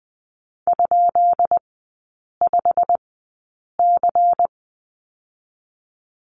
Morse code, heard as a keyed tone, about 20 wpm, spaced as for 8 wpm, 690 Hz. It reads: ?5C
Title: 205 Lesson 33e - Letter j - Three Characters 20wpm